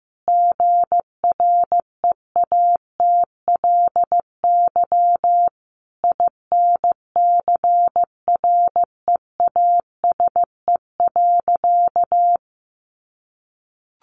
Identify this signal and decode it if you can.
Morse code, heard as a keyed tone, about 15 wpm, 700 Hz. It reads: GREATLY INCREASE.
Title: Sentences from Top 1400 Words 15wpm